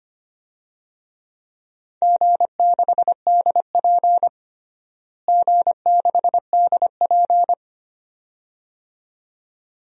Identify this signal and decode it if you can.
Morse code, heard as a keyed tone, about 25 wpm, 700 Hz. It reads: G6DP G6DP
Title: DXCC-100 Call Signs - 4 Characters - Repeated 2x - 25wpm